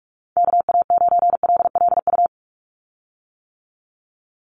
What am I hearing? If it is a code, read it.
Morse code, heard as a keyed tone, about 44 wpm, 700 Hz. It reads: KA9LLU
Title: USA Call Signs 45wpm